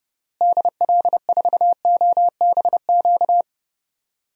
Morse code, heard as a keyed tone, about 30 wpm, 700 Hz. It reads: DL4OBQ
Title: USA Call Signs - 6 Characters - Repeated 3x - 30wpm